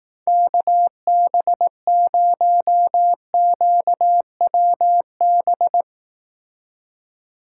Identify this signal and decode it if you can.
Morse code, heard as a keyed tone, about 18 wpm, 700 Hz. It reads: KB0QWB
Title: Speed Racing - USA 2x3 Call Signs 15wpm